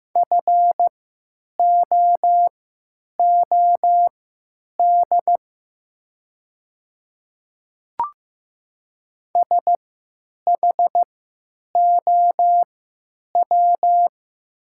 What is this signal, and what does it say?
Morse code, heard as a keyed tone, about 16 wpm, spaced as for 9 wpm, 700 Hz. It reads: FOOD SHOW
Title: 3x Character Spacing - Top 500 Words - 15wpm